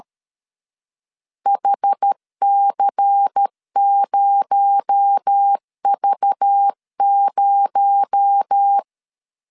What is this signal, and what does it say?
Morse code, heard as a keyed tone, about 13 wpm, 800 Hz. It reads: HC0V0
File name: LEZIONI / 6 PKX0 / 6 PKX0 12wpm 2231